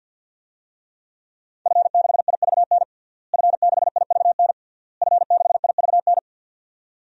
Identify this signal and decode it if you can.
Morse code, heard as a keyed tone, about 52 wpm, 700 Hz. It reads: F6IVN F6IVN F6IVN
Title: DXCC-100 Call Signs - 5 Characters - Repeated 3x - 50wpm